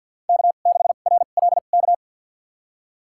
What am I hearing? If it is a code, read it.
Morse code, heard as a keyed tone, about 47 wpm, spaced as for 35 wpm, 700 Hz. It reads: K6RLK
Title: USA Call Signs - 5 Characters - 2x Character Spacing - 50wpm